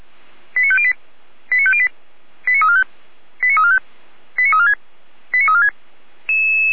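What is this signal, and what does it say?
Five-tone selective call (ZVEI), heard as quick runs of five tones, tones of about 70 ms. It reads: 89508 89508 89345 89345 89346 89346
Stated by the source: ZVEI five-tone selective call received by SDR.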